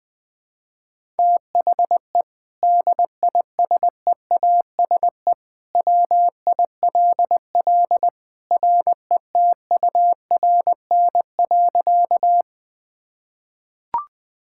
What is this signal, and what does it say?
Morse code, heard as a keyed tone, about 19 wpm, 700 Hz. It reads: THE DISEASE WILL RETURN.
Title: Sentences from Top 1000 Words - Review Part 6 - 20wpm